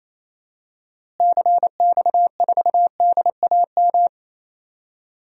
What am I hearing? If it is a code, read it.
Morse code, heard as a keyed tone, about 28 wpm, 700 Hz. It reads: CX4DAM
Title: DXCC-100 Call Signs - 6 Characters - Repeated 2x - 28wpm